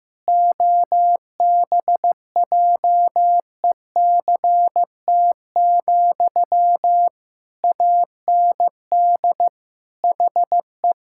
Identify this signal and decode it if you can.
Morse code, heard as a keyed tone, about 16 wpm, 700 Hz. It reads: OBJECT, AND HE
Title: Sentences from Top 1000 Words - Review Part 6 - 15wpm